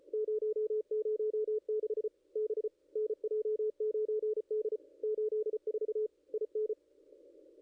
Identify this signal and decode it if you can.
Morse code, heard as a keyed tone, about 33 wpm, 440 Hz, with radio noise. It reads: 006 B NJ9D 84 IN
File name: contests / ss-cw-k5tr-op-n5rz -0500